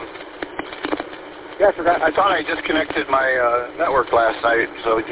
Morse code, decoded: none

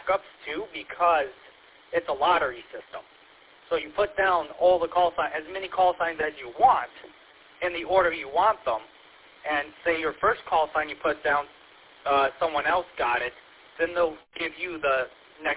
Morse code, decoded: none

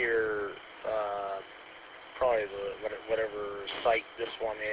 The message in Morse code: none

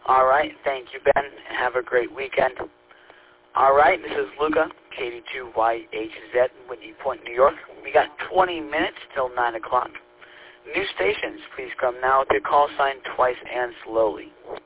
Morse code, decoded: none

